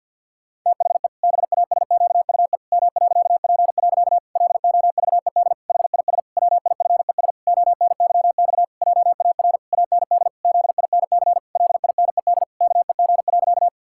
Morse code, heard as a keyed tone, about 50 wpm, 700 Hz. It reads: THE BASQUE M.P. LOVED HIS WIFES ONYX JAR AND ZINC LINED KEG.
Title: Pangram Challenge v2 - 50wpm